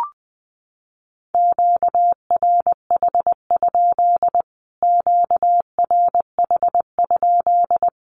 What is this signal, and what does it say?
Morse code, heard as a keyed tone, about 20 wpm, 700 Hz. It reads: QRH? QRH?
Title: Q-Codes Repeated 3x 20wpm